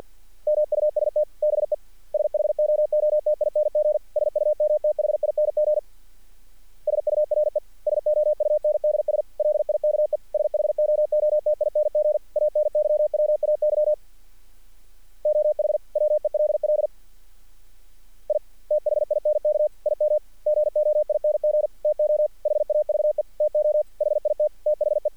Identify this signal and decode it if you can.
Morse code, heard as a keyed tone, about 50 wpm, 600 Hz. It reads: MUST BE SHOOTING SUMTHING SURE SOUNDS LIKE SHOOTING ANYWAY OH WELL I THINK IM GOING TO HAVE TO HIT THE
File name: archived / CRSnet